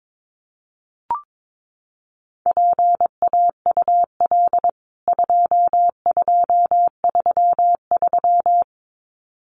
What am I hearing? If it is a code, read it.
Morse code, heard as a keyed tone, about 22 wpm, 700 Hz. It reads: PAUL 2233
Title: CWT Exchange 22wpm